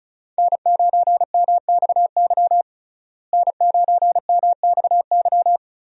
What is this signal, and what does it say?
Morse code, heard as a keyed tone, about 36 wpm, 700 Hz. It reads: N9MXY N9MXY
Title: USA Call Signs - 5 Characters - Repeated 3x - 35wpm